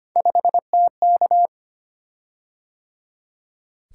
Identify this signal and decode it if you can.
Morse code, heard as a keyed tone, about 25 wpm, 700 Hz. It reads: 5TK
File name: Three Letter-Number Combinations - 25wpm